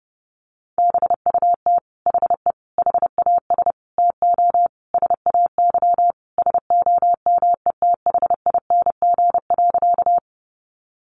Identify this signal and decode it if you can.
Morse code, heard as a keyed tone, about 30 wpm, 700 Hz. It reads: BUT HE HAS TO SAY SOMETHING.
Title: Sentences from Top 600 Words 30wpm